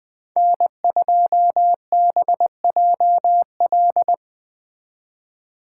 Morse code, heard as a keyed tone, about 20 wpm, 700 Hz. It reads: N2BJL
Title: Speed Racing - USA Call Signs 17wpm